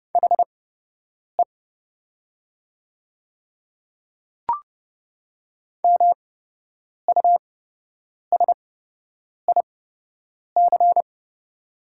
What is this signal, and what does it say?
Morse code, heard as a keyed tone, about 30 wpm, spaced as for 8 wpm, 700 Hz. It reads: HE MUSIC
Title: 8x Character Spacing - Top 300 Words - 30wpm